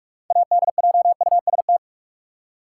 Morse code, heard as a keyed tone, about 46 wpm, 700 Hz. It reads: ADJUST
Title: Six Letter Words - 45wpm